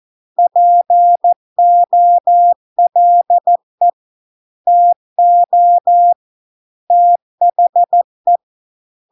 Morse code, heard as a keyed tone, about 14 wpm, 700 Hz. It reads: POLE TO THE